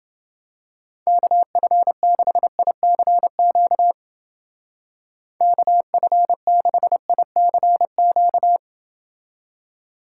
Morse code, heard as a keyed tone, about 28 wpm, 700 Hz. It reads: KF6ICQ KF6ICQ